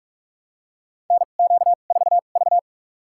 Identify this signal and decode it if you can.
Morse code, heard as a keyed tone, about 46 wpm, spaced as for 33 wpm, 700 Hz. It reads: NQ4V
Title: USA Call Signs - 4 Characters - 2x Character Spacing - 45wpm